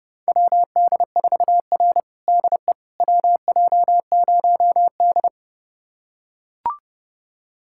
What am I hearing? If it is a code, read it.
Morse code, heard as a keyed tone, about 30 wpm, 700 Hz. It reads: WD4R DE WJ0D
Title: USA Callsign DE Callsign - 4 Characters - 30wpm